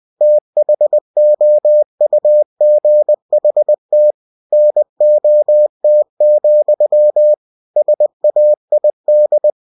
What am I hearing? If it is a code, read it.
Morse code, heard as a keyed tone, about 20 wpm, 600 Hz. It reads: THOUGHT NOT, SAID
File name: cw-books / winnie-the-pooh / winnie-20wpm-page-0016